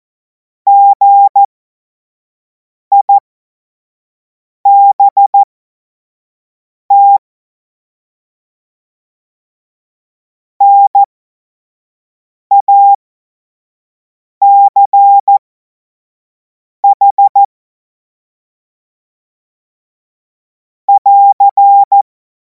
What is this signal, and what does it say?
Morse code, heard as a keyed tone, about 14 wpm, spaced as for 5 wpm, 800 Hz. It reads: GIBT NACH <AR>
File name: Morsekurs / Witze / Witz 19. Mai 2026 25 F70